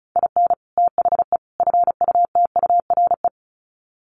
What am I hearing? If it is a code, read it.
Morse code, heard as a keyed tone, about 34 wpm, 700 Hz. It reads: IN THE FUTURE
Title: Sets of 3 Words 35wpm